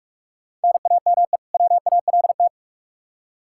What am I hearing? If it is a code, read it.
Morse code, heard as a keyed tone, about 46 wpm, 700 Hz. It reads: NAME WALT
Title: Practice QSO - Exchange 1 - Beginning - 45wpm